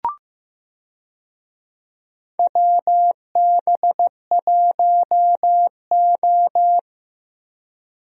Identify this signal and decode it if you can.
Morse code, heard as a keyed tone, about 15 wpm, 700 Hz. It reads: WB1O
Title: USA Call Signs - 4 Characters - Repeated 3x - 15wpm